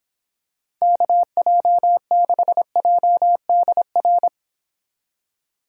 Morse code, heard as a keyed tone, about 26 wpm, 700 Hz. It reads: KJ6JDR